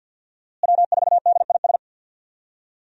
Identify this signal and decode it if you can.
Morse code, heard as a keyed tone, about 50 wpm, 700 Hz. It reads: W4DIS
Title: USA Call Signs - 5 Characters - Repeated 3x - 50wpm